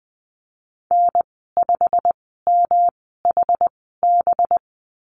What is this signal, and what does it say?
Morse code, heard as a keyed tone, about 20 wpm, spaced as for 15 wpm, 700 Hz. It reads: N5MHB
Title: 193 Lesson 31g - Letter v - Call Signs 20wpm